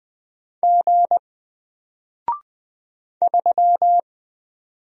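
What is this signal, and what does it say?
Morse code, heard as a keyed tone, about 20 wpm, 700 Hz. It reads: G 3
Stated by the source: Single letter and number Morse drill at 20 wpm, from N4LCD.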